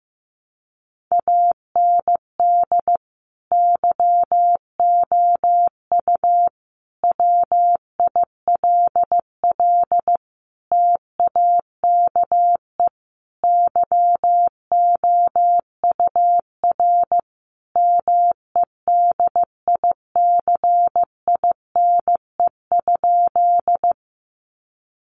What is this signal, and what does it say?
Morse code, heard as a keyed tone, about 15 wpm, 700 Hz. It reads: AND YOU WILL TAKE YOUR MEDICINE?
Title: Sentences from Top 900 Words 15wpm